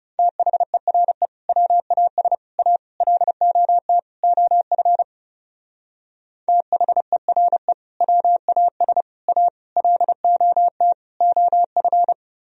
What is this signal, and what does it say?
Morse code, heard as a keyed tone, about 36 wpm, 700 Hz. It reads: THERE WAS A LOT OF THERE WAS A LOT OF
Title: Speed-Racing - Sets of 5 Words 30wpm